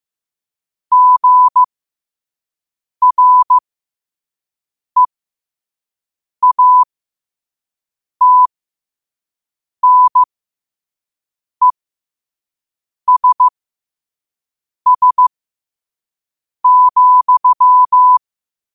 Morse code, heard as a keyed tone, about 15 wpm, spaced as for 5 wpm, 1000 Hz. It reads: GREATNESS,
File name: qotdfiles / 5wpm